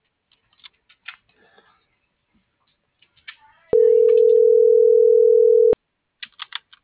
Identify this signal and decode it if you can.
Ringback tone (North America), 440 + 480 Hz.
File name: CENTRAL FRALSE / out brendaaviles.j